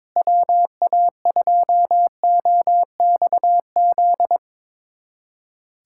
Morse code, heard as a keyed tone, about 22 wpm, 700 Hz. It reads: WA2OXZ
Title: USA Callsign DE Callsign - 6 Characters - 22wpm